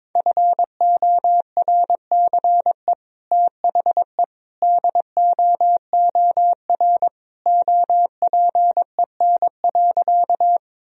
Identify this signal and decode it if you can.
Morse code, heard as a keyed tone, about 22 wpm, 700 Hz. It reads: FORCE THE DOOR OPEN.
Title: Sentences from Top 1000 Words - Review Part 3 - 22wpm